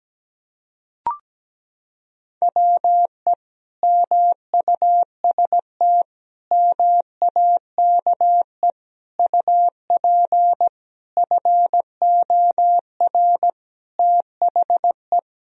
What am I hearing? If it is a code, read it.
Morse code, heard as a keyed tone, about 17 wpm, 700 Hz. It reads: WE MUST MAKE UP FOR THE
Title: Sentences from Top 1000 Words - Review Part 3 - 17wpm